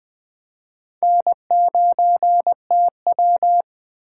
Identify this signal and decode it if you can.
Morse code, heard as a keyed tone, about 21 wpm, 700 Hz. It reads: N9TW
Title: Speed Racing - USA 1x2 Call Signs 17wpm